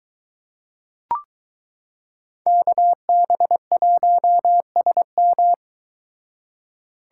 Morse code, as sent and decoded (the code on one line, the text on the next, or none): -.- -... .---- ... --
KB1SM